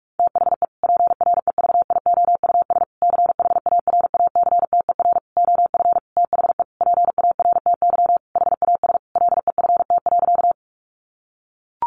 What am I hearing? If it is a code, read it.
Morse code, heard as a keyed tone, about 44 wpm, 700 Hz. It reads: THE PREVIOUS CHARACTER OF THE PARTY HAS LEFT.
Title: Sentences from Top 1500 Words 45wpm